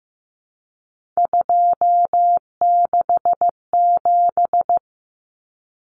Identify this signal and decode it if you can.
Morse code, heard as a keyed tone, about 15 wpm, 700 Hz. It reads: 267
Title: Common Cut Numbers - 3-Digit 15wpm